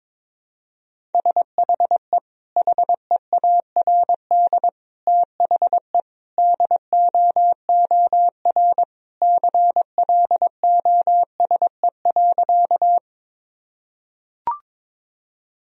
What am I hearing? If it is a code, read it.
Morse code, heard as a keyed tone, about 22 wpm, 700 Hz. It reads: SHE HEARD THE DOOR CLOSE.
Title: Sentences from Top 1000 Words - Review Part 1 - 22wpm